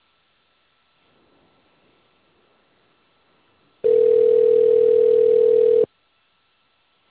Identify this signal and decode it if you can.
Ringback tone (North America), 440 + 480 Hz.